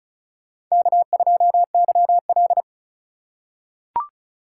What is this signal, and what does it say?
Morse code, heard as a keyed tone, about 36 wpm, 700 Hz. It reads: K2YL
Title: USA Call Signs - 3x Character Spacing - 35wpm